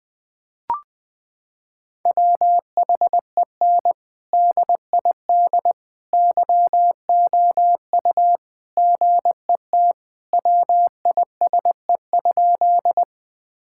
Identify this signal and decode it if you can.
Morse code, heard as a keyed tone, about 20 wpm, 700 Hz. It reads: WHEN DID YOU GET WISE?
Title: Sentences from Top 1600 Words 20wpm